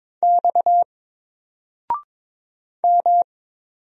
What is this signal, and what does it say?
Morse code, heard as a keyed tone, about 22 wpm, 700 Hz. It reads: X M
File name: Single-Letter-Number-22wpm